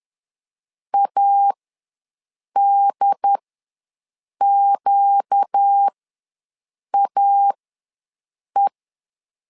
Morse code, heard as a keyed tone, about 10 wpm, spaced as for 6 wpm, 800 Hz. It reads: ADQAE
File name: LEZIONI / 4 Q ws 10wpm 2306